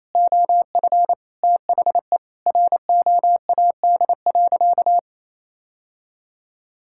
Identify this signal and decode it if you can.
Morse code, heard as a keyed tone, about 28 wpm, 700 Hz. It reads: OF THE ROAD.
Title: Sentences from Top 1000 Words - Review Part 5 - 28wpm